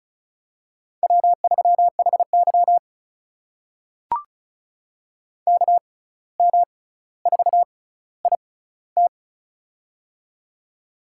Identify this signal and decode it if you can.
Morse code, heard as a keyed tone, about 36 wpm, spaced as for 17 wpm, 700 Hz. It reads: W3HY KM4IT
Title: USA Call Signs - 6x Character Spacing - 35wpm